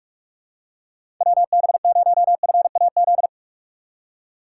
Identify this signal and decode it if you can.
Morse code, heard as a keyed tone, about 46 wpm, 700 Hz. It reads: WB0FAZ
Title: USA Call Signs - 6 Characters - 6x Character Spacing - 45wpm